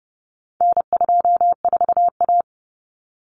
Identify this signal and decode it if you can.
Morse code, heard as a keyed tone, about 30 wpm, 700 Hz. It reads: N24A
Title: Common Cut Numbers - 4-Digit 30wpm